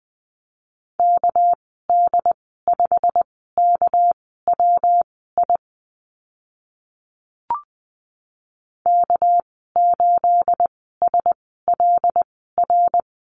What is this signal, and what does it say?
Morse code, heard as a keyed tone, about 20 wpm, spaced as for 14 wpm, 700 Hz. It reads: KD5KWI K8SLR
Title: 213 Lesson 34g - Number 8 - Call Signs 20wpm